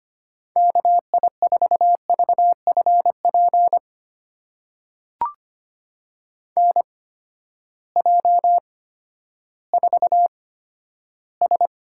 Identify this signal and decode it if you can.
Morse code, heard as a keyed tone, about 25 wpm, spaced as for 12 wpm, 700 Hz. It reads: KI4VFP NJ4S
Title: USA Call Signs - 8x Character Spacing - 25wpm